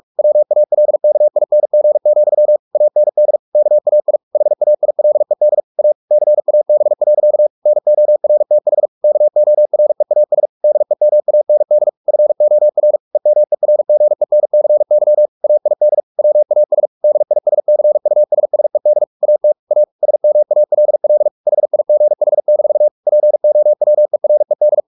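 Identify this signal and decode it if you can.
Morse code, heard as a keyed tone, about 44 wpm, 600 Hz. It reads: WALKING, AND KAI HAILED A CAB. NORTH KOREAS DEMAND FOR EMERGENCY AID WAS DISCUSSED AT A SMALL HIGH-POWERED
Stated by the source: Morse practice at 45 wpm, the novel Never by Ken Follett in English, from CW med Gnister.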